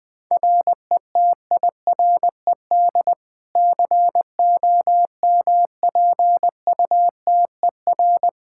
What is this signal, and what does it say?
Morse code, heard as a keyed tone, about 20 wpm, 700 Hz. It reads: RETIRED COMPUTER
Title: Practice QSO - Exchange 3 - More Info - 20wpm